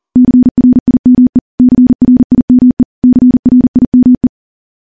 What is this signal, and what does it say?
Morse code, heard as a keyed tone, about 43 wpm, 250 Hz. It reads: CRIME CRIME CRIME
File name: morse / morse-training-40WPM